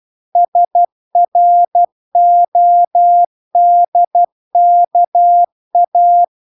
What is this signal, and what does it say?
Morse code, heard as a keyed tone, about 12 wpm, 700 Hz. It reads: SRODKA